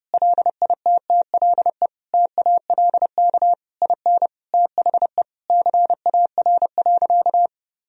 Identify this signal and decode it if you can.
Morse code, heard as a keyed tone, about 30 wpm, 700 Hz. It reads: LITTLE TALK IN THE CAR.
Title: Sentences from Top 1000 Words - Review Part 1 - 30wpm